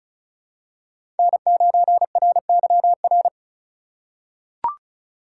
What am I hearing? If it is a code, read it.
Morse code, heard as a keyed tone, about 36 wpm, 690 Hz. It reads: N9RYR
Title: USA Call Signs - 5 Characters - 3x Character Spacing - 35wpm